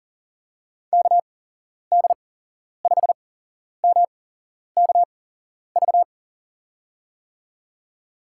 Morse code, heard as a keyed tone, about 40 wpm, spaced as for 11 wpm, 700 Hz. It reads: KD5MKV
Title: USA Call Signs - 8x Character Spacing - 40wpm